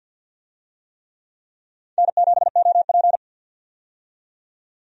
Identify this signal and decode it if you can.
Morse code, heard as a keyed tone, about 51 wpm, 700 Hz. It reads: N7OP
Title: USA Call Signs - 4 Characters - 50wpm